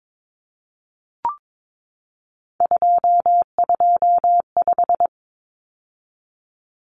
Morse code, heard as a keyed tone, about 22 wpm, 700 Hz. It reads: 225
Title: Common Cut Numbers - 3-Digit 22wpm